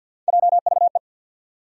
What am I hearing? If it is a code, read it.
Morse code, heard as a keyed tone, about 54 wpm, 700 Hz. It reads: JVE
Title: USA Call Sign Postfixes - 3 Letters 50wpm